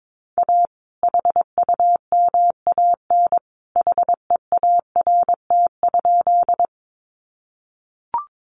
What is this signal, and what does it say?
Morse code, heard as a keyed tone, about 22 wpm, 700 Hz. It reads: A HUMAN HEART?
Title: Sentences from Top 1200 Words 22wpm